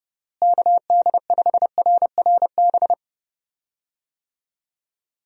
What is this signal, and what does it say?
Morse code, heard as a keyed tone, about 30 wpm, 700 Hz. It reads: KD5RRB
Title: USA Call Signs - 6 Characters - Repeated 2x - 30wpm